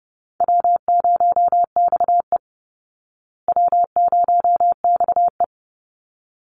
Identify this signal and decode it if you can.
Morse code, heard as a keyed tone, about 28 wpm, 700 Hz. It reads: W0XE W0XE
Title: Easy Call Signs Repeated 2x - 30wpm